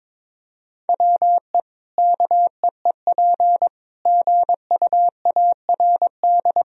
Morse code, heard as a keyed tone, about 22 wpm, 700 Hz. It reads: WE KEEP GUARD
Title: Sentences from Top 1000 Words - Review Part 4 - 22wpm